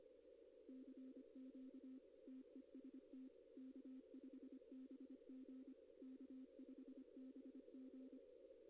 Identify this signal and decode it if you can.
Morse code, heard as a keyed tone, about 25 wpm, 260 Hz. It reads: CQ TEST K5BG K5BG
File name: contests / arrl-160-k5tr-op-w5tn -2200